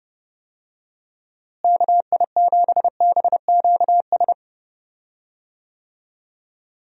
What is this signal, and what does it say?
Morse code, heard as a keyed tone, about 31 wpm, 700 Hz. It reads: KI7BQS
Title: USA Call Signs - 6 Characters - Repeated 3x - 30wpm